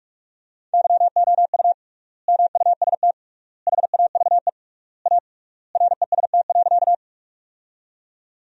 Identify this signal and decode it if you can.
Morse code, heard as a keyed tone, about 46 wpm, 700 Hz. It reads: YOU MUST HAVE A REST.
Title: 3x Word-Spacing - Sentences from Top 500 Words - 45wpm